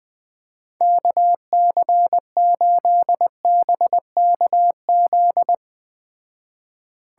Morse code, heard as a keyed tone, about 20 wpm, 700 Hz. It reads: KC8BKZ
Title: Speed Racing - USA Call Signs 17wpm